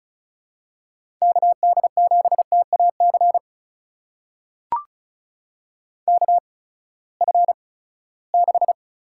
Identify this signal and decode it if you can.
Morse code, heard as a keyed tone, about 36 wpm, spaced as for 20 wpm, 700 Hz. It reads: KD7TAC KF6
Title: USA Call Signs - 6 Characters - 8x Character Spacing - 35wpm